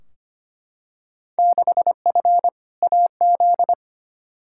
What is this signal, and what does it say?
Morse code, heard as a keyed tone, about 25 wpm, 700 Hz. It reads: 6F AZ
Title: Field Day Exchanges Encore - 25wpm